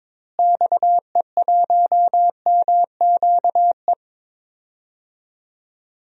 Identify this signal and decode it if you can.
Morse code, heard as a keyed tone, about 21 wpm, 700 Hz. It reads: XE1MQE
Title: DXCC-100 Call Signs - 6 Characters - Repeated 3x - 22wpm